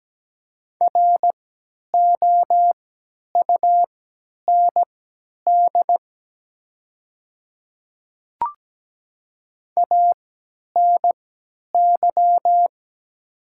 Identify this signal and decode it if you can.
Morse code, heard as a keyed tone, about 17 wpm, spaced as for 10 wpm, 700 Hz. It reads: ROUND ANY
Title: 3x Character Spacing - Top 300 Words - 17wpm